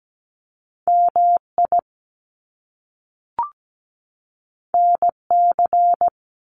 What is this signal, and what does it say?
Morse code, heard as a keyed tone, about 17 wpm, 700 Hz. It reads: MI NC
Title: US States 17wpm